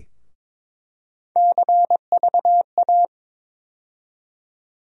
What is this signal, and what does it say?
Morse code, heard as a keyed tone, about 21 wpm, 700 Hz. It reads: CVA
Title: USA Call Sign Postfixes - 3 Letters 22wpm